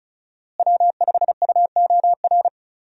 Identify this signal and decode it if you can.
Morse code, heard as a keyed tone, about 36 wpm, 700 Hz. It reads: W5UOR
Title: USA Call Signs - 5 Characters - Repeated 3x - 35wpm